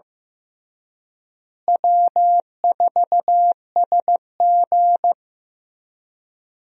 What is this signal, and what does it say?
Morse code, heard as a keyed tone, about 15 wpm, 700 Hz. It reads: W4SG
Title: USA Call Signs - 4 Characters - Repeated 2x - 15wpm